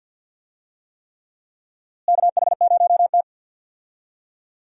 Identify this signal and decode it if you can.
Morse code, heard as a keyed tone, about 51 wpm, 700 Hz. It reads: KH0T
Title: USA Call Signs - 4 Characters - Repeated 3x - 50wpm